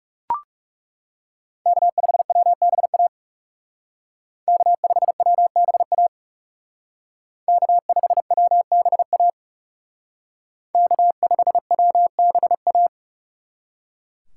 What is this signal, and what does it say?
Morse code, heard as a keyed tone, about 37 wpm, 700 Hz. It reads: K5WBA K5WBA K5WBA K5WBA
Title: Speed Racing - USA 1x3 Call Signs 30wpm